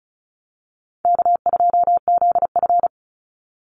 Morse code, heard as a keyed tone, about 34 wpm, 700 Hz. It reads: K2ZF
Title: Call Signs - 35wpm